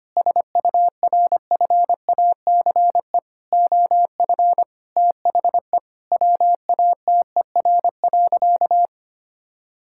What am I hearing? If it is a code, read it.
Morse code, heard as a keyed tone, about 25 wpm, 700 Hz. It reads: SURFACE OF THE WATER.
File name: practice / 25wpm / Sentences with 4-Word Phrases - 25wpm